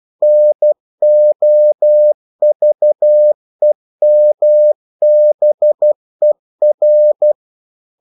Morse code, heard as a keyed tone, about 12 wpm, 600 Hz. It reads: NOVEMBER